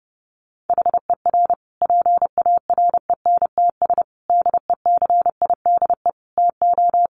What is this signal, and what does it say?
Morse code, heard as a keyed tone, about 30 wpm, 700 Hz. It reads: HER PARENTS DECIDE TO
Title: Easy Phrases 30wpm